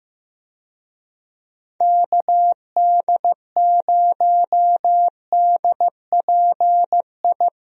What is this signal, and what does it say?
Morse code, heard as a keyed tone, about 15 wpm, 700 Hz. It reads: KD0DPI
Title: USA Call Signs - 6 Characters - Repeated 2x - 15wpm